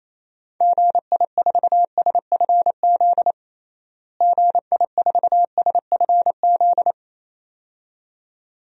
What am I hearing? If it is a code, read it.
Morse code, heard as a keyed tone, about 28 wpm, 700 Hz. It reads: GI4SFZ GI4SFZ
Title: DXCC-100 Call Signs - 6 Characters - Repeated 2x - 28wpm